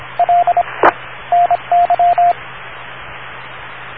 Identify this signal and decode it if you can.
Morse code, heard as a keyed tone, about 25 wpm, spaced as for 22 wpm, 700 Hz, with radio noise. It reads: LE NY